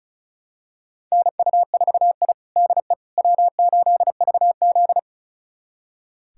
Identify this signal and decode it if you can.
Morse code, heard as a keyed tone, about 36 wpm, 700 Hz. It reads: NU4I DE W8VZ
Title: USA Callsign DE Callsign - 4 Characters - 35wpm